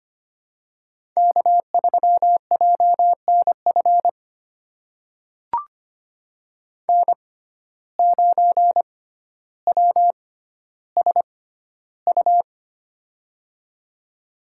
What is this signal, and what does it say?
Morse code, heard as a keyed tone, about 25 wpm, spaced as for 13 wpm, 700 Hz. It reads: K3JNF N9WSU
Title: USA Call Signs - 5 Characters - 6x Character Spacing - 25wpm